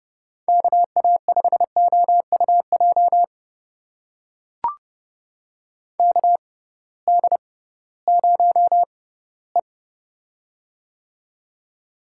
Morse code, heard as a keyed tone, about 30 wpm, spaced as for 18 wpm, 700 Hz. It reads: KA5OUJ KD0E
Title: USA Call Signs - 6x Character Spacing - 30wpm